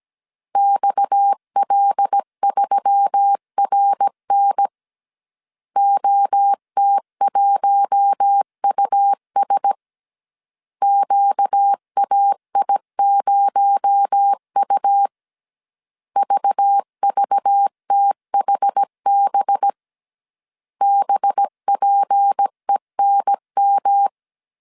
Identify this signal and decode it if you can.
Morse code, heard as a keyed tone, about 17 wpm, 800 Hz. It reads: XL3RN OT1US QAI0U VVTHB BPENM
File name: CW / DET / 1 6 1500C S A 16wpm 2635